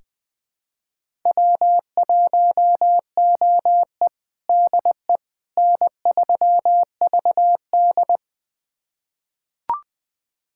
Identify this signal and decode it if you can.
Morse code, heard as a keyed tone, about 20 wpm, 700 Hz. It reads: W1OE DE N3VD
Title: USA Callsign DE Callsign - 4 Characters - 20wpm